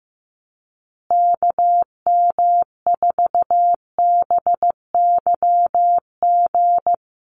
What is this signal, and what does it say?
Morse code, heard as a keyed tone, about 15 wpm, 700 Hz. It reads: KM4BYG
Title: USA Call Signs Repeated 3x 15wpm